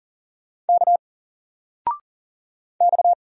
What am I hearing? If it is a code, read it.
Morse code, heard as a keyed tone, about 40 wpm, 700 Hz. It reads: K X
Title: Single Letter-Number 40wpm